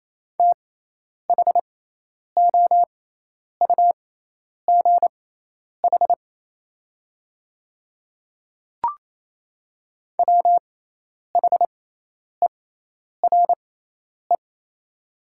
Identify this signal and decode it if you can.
Morse code, heard as a keyed tone, about 28 wpm, spaced as for 10 wpm, 700 Hz. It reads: THOUGH WHERE
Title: 6x Character Spacing - Top 500 Words - 28wpm